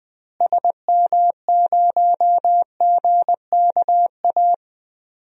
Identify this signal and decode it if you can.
Morse code, heard as a keyed tone, about 20 wpm, 700 Hz. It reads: SM0GKA
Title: DXCC-100 Call Signs - 6 Characters - Repeated 3x - 20wpm